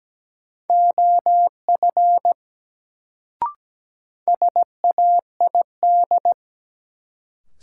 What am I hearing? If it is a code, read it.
Morse code, heard as a keyed tone, about 17 wpm, 700 Hz. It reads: OF SAID
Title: Sentences from Top 100 Words - New Word Drill - 17wpm